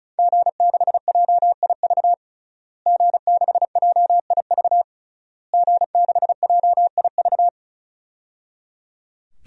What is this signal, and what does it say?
Morse code, heard as a keyed tone, about 36 wpm, 700 Hz. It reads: G6JIV G6JIV G6JIV
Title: DXCC-100 Call Signs - 5 Characters - Repeated 3x - 35wpm